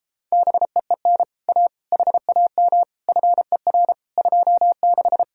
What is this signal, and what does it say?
Morse code, heard as a keyed tone, about 29 wpm, 700 Hz. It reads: BEEN A HAM FER 26